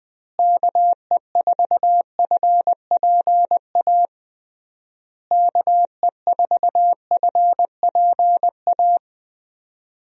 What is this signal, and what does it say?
Morse code, heard as a keyed tone, about 20 wpm, 700 Hz. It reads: KE4FPA KE4FPA
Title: USA Call Signs - 6 Characters - Repeated 3x - 20wpm